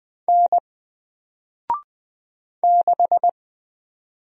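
Morse code, heard as a keyed tone, about 20 wpm, 700 Hz. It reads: N 6
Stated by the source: Single letter and number Morse drill at 20 wpm, from N4LCD.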